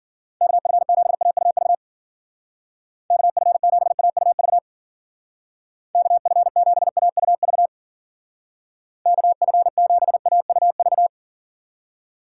Morse code, heard as a keyed tone, about 51 wpm, 700 Hz. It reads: KF7AUV KF7AUV KF7AUV KF7AUV